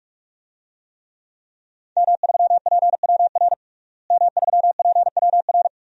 Morse code, heard as a keyed tone, about 46 wpm, 700 Hz. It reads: M3PWR M3PWR
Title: DXCC-100 Call Signs - 5 Characters - Repeated 2x - 45wpm